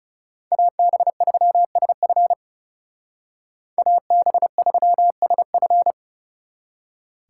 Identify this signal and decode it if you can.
Morse code, heard as a keyed tone, about 37 wpm, 700 Hz. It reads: AB3SF AB3SF
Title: Speed Racing - USA 2x2 Call Signs 30wpm